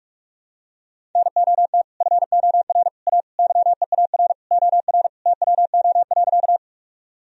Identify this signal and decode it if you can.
Morse code, heard as a keyed tone, about 41 wpm, 700 Hz. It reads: NOT FOR A YEAR OR TWO.
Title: Sentences from Top 1000 Words - Review Part 1 - 45wpm